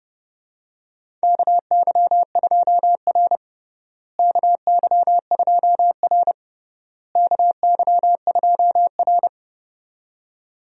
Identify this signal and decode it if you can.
Morse code, heard as a keyed tone, about 32 wpm, 700 Hz. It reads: KY2R KY2R KY2R
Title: USA Call Signs - 4 Characters - Repeated 3x - 30wpm